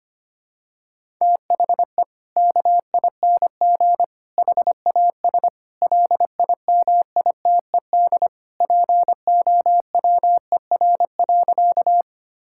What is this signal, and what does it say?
Morse code, heard as a keyed tone, about 25 wpm, 700 Hz. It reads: THE KING HAS LIMITED POWER.
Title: Sentences from Top 1000 Words - Review Part 6 - 25wpm